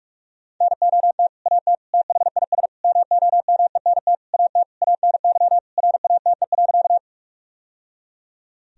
Morse code, heard as a keyed tone, about 45 wpm, 700 Hz. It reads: NOT AT THIS MOMENT AT ANY RATE.
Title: Sentences from Top 1000 Words - Review Part 4 - 45wpm